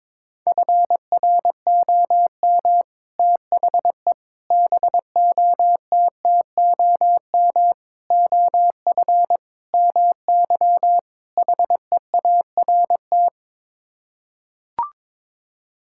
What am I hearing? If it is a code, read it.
Morse code, heard as a keyed tone, about 22 wpm, 700 Hz. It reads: FROM THE BOTTOM OF MY HEART
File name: practice / 22wpm / Sets of 6 Words - 22wpm